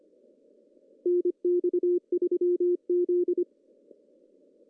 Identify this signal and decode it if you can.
Morse code, heard as a keyed tone, about 25 wpm, 350 Hz. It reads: NX3Z
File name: CW / cqww-cw-k5tr